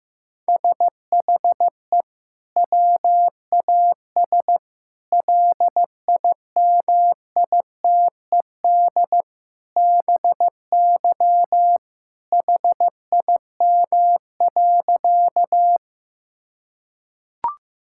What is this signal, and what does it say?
Morse code, heard as a keyed tone, about 15 wpm, 700 Hz. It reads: SHE WAS LIMITED BY HIM.
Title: Sentences from Top 1000 Words - Review Part 6 - 15wpm